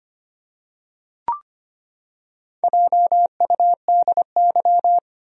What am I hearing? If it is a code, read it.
Morse code, heard as a keyed tone, about 25 wpm, 700 Hz. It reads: JUDY
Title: USA Common Names - Max 5 Letters - 25wpm